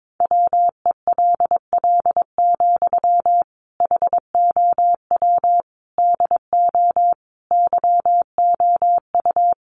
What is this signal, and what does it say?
Morse code, heard as a keyed tone, about 22 wpm, 700 Hz. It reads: WELL, HOW DO YOU
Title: Sentences from Top 1100 Words 22wpm